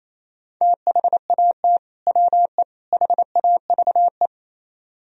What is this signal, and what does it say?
Morse code, heard as a keyed tone, about 36 wpm, 700 Hz. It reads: THAT WE HAVE